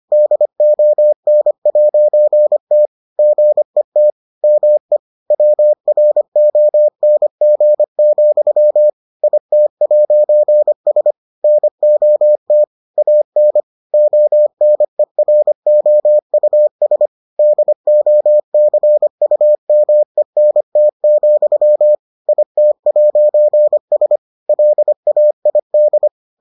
Morse code, heard as a keyed tone, about 25 wpm, 600 Hz. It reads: DON'T GET ME WRONG, IT'S NOT AN ONEROUS DOCUMENT, IT'S LAID